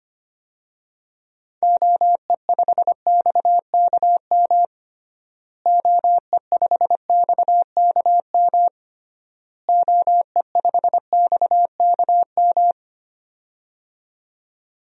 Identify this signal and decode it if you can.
Morse code, heard as a keyed tone, about 25 wpm, 700 Hz. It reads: OE5XKM OE5XKM OE5XKM
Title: DXCC-100 Call Signs - 6 Characters - Repeated 3x - 25wpm